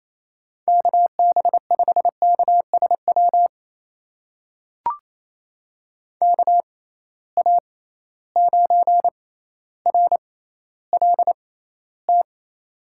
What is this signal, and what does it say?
Morse code, heard as a keyed tone, about 28 wpm, spaced as for 14 wpm, 700 Hz. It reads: KB5KSW KA9RLT
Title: USA Call Signs - 6 Characters - 6x Character Spacing - 28wpm